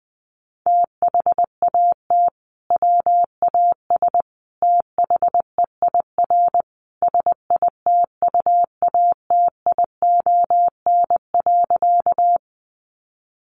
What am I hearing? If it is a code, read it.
Morse code, heard as a keyed tone, about 19 wpm, 700 Hz. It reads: THAT WAS THEIR SITUATION.
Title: Sentences from Top 1200 Words 20wpm